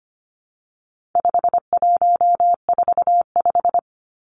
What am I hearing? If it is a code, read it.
Morse code, heard as a keyed tone, about 25 wpm, 700 Hz. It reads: 5145
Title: Numbers - 4-Digit 25wpm